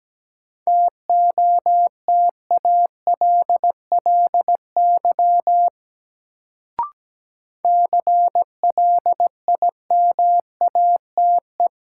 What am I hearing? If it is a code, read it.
Morse code, heard as a keyed tone, about 17 wpm, 700 Hz. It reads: TOTALLY CLIMATE
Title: Sentences from Top 1600 Words - New Word Drill - 17wpm